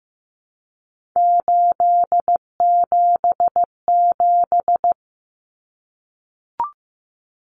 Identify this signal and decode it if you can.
Morse code, heard as a keyed tone, about 15 wpm, 700 Hz. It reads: 877
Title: Common Cut Numbers - 3-Digit 15wpm